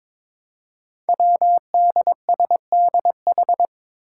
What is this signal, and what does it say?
Morse code, heard as a keyed tone, about 22 wpm, 700 Hz. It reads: WDSDH
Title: Five Challenging Letter-Number Combinations 22wpm